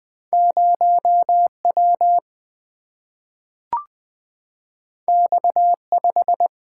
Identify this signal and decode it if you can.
Morse code, heard as a keyed tone, about 20 wpm, 700 Hz. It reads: 0 W X 5
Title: Single-Letter Single-Number 20wpm